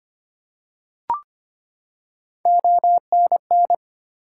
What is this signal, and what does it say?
Morse code, heard as a keyed tone, about 25 wpm, 700 Hz. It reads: ONN
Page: morse-code-ninja-practice-25wpm